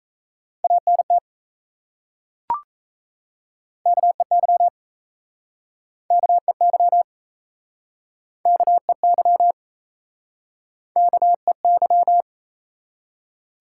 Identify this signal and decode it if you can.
Morse code, heard as a keyed tone, about 36 wpm, 700 Hz. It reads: ANT KEY KEY KEY KEY